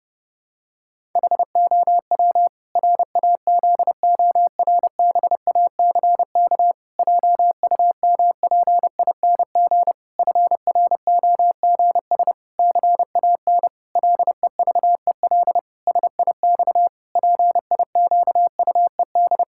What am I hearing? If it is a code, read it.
Morse code, heard as a keyed tone, about 32 wpm, 700 Hz. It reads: HOW RAZORBACK JUMPING FROGS CAN LEVEL SIX PIQUED
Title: Pangram Challenge v2 - 30wpm